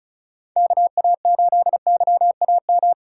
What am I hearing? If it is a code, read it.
Morse code, heard as a keyed tone, about 36 wpm, 700 Hz. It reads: KA8YAM
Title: USA Callsign DE Callsign - 6 Characters - 35wpm